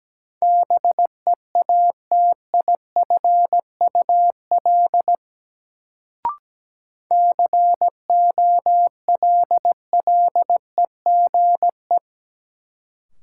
Morse code, heard as a keyed tone, about 17 wpm, 700 Hz. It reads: BEATIFUL COLLEGE
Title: Sentences from Top 1300 Words - New Word Drill Alt - 17wpm